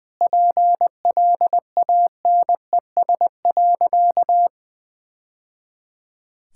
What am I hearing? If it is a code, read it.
Morse code, heard as a keyed tone, about 21 wpm, 700 Hz. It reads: PLANES.
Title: Easy Sentences with Hints - 20wpm